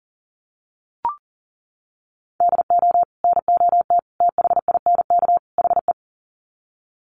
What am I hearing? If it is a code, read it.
Morse code, heard as a keyed tone, about 40 wpm, 700 Hz. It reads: DO NOT THINK HE
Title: Sets of 4 Words 40wpm